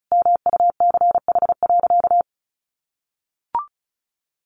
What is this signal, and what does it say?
Morse code, heard as a keyed tone, about 34 wpm, 700 Hz. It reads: MUCH.
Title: Sentences from Top 500 Words 35wpm